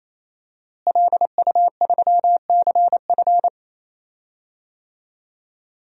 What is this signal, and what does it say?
Morse code, heard as a keyed tone, about 28 wpm, 700 Hz. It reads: LU3CF
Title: DXCC-100 Call Signs - 5 Characters - Repeated 3x - 28wpm